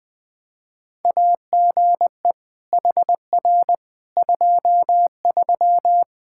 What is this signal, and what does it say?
Morse code, heard as a keyed tone, about 20 wpm, 700 Hz. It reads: AGE HR 23
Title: K1USN SST QSO Practice 20wpm